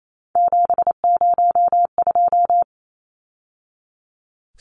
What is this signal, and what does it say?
Morse code, heard as a keyed tone, about 28 wpm, 700 Hz. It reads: 702
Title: Numbers - 3-Digit 28wpm